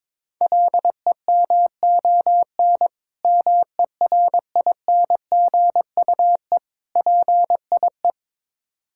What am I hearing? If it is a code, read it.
Morse code, heard as a keyed tone, about 22 wpm, 700 Hz. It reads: LEMON MERINGUE PIE
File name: Difficult Words with Hints - 22wpm